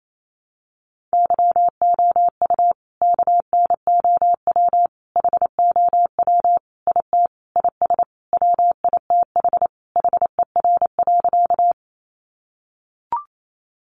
Morse code, heard as a keyed tone, about 28 wpm, 700 Hz. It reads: YOU KNOW HOW IT IS WITH HER.
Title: Sentences from Top 100 Words 28wpm